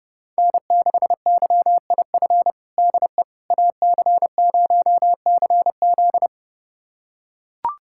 Morse code, heard as a keyed tone, about 30 wpm, 700 Hz. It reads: N6YIF DE AC0CZ
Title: USA Callsign DE Callsign - 5 Characters - 30wpm